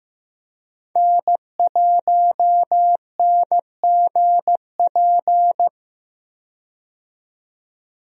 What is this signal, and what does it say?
Morse code, heard as a keyed tone, about 15 wpm, 700 Hz. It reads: N1NGP
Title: USA Call Signs - 5 Characters - 15wpm